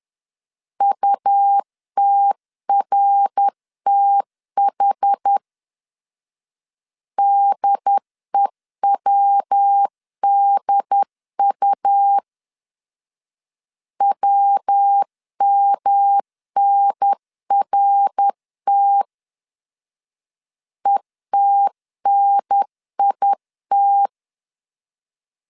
Morse code, heard as a keyed tone, about 10 wpm, 800 Hz. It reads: UTRTH DEWDU WMNRT ETNIT
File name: 10wpm